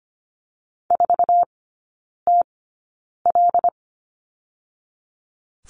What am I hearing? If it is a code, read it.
Morse code, heard as a keyed tone, about 25 wpm, spaced as for 9 wpm, 690 Hz. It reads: 4TL
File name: Lesson 14e - Number 2 - Three Characters 25wpm